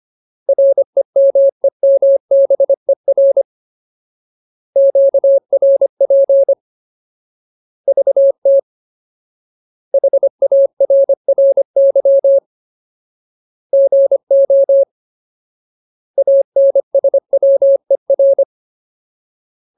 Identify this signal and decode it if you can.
Morse code, heard as a keyed tone, about 25 wpm, 550 Hz. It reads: REMEMBER QRP VT HARRY GO ANSWER
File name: Ultimate Word List / 25WPM Ultimate